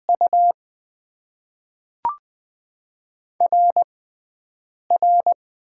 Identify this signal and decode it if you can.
Morse code, heard as a keyed tone, about 20 wpm, spaced as for 3 wpm, 700 Hz. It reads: UR R
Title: Top 90 QSO Elements - 6x Character Spacing - 20wpm